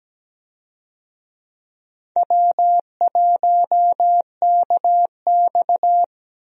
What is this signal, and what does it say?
Morse code, heard as a keyed tone, about 17 wpm, 700 Hz. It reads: W1KX
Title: USA Call Signs - 4 Characters - Repeated 2x - 17wpm